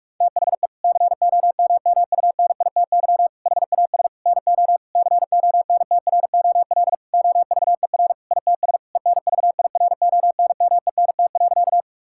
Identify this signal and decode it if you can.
Morse code, heard as a keyed tone, about 46 wpm, 700 Hz. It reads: THE COMMUNITY HAS NO CONTROL OVER ITS ENVIRONMENT.
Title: Sentences from Top 1000 Words - Review Part 5 - 45wpm